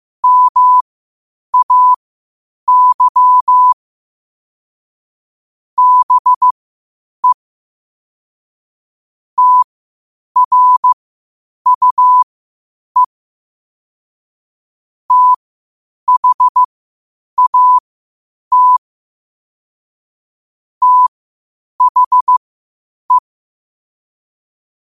Morse code, heard as a keyed tone, about 15 wpm, spaced as for 8 wpm, 1000 Hz. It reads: MAY BE TRUE THAT THE
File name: qotdfiles / 8wpm